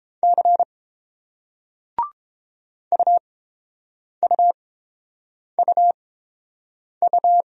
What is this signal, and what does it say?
Morse code, heard as a keyed tone, about 29 wpm, spaced as for 7 wpm, 700 Hz. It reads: C UUUU